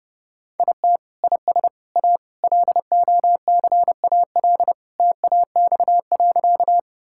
Morse code, heard as a keyed tone, about 30 wpm, 700 Hz. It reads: IT IS A LOCAL TAX.
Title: Sentences from Top 1000 Words - Review Part 5 - 30wpm